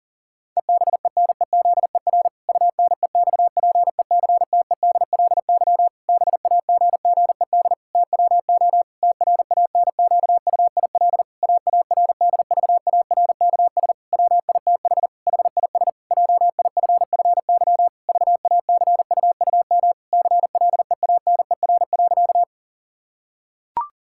Morse code, heard as a keyed tone, about 40 wpm, 700 Hz. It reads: EBENEZER UNEXPECTEDLY BAGGED TWO TRANQUIL AARDVARKS WITH HIS JIFFY VACUUM CLEANER.
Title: Pangram Challenge v2 - 40wpm